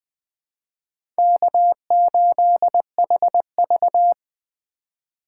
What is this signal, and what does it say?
Morse code, heard as a keyed tone, about 20 wpm, 700 Hz. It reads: K8HV
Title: USA Call Signs - 4 Characters - 3x Character Spacing - 20wpm